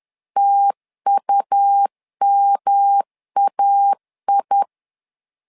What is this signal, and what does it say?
Morse code, heard as a keyed tone, about 10 wpm, 800 Hz. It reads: TUMAI
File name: LEZIONI / 3 MWRO / 3 MW 10wpm 1959